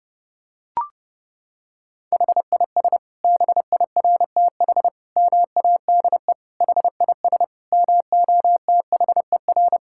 Morse code, heard as a keyed tone, about 30 wpm, 700 Hz. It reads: HIS BIRTH MADE HIS MOTHER
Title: Sentences from Top 1000 Words - Review Part 5 - 30wpm